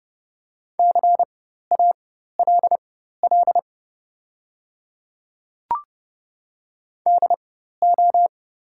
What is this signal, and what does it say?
Morse code, heard as a keyed tone, about 30 wpm, spaced as for 14 wpm, 700 Hz. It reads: CALL DO
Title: 4x Character Spacing - Top 100 Words - 30wpm